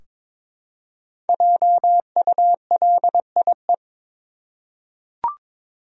Morse code, heard as a keyed tone, about 22 wpm, 700 Hz. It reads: JULIE
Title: USA Common Names - Max 5 Letters - 22wpm